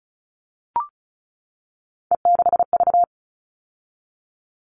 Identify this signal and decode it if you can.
Morse code, heard as a keyed tone, about 34 wpm, 700 Hz. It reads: E6V
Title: All Cut Numbers - 3-Digit 35wpm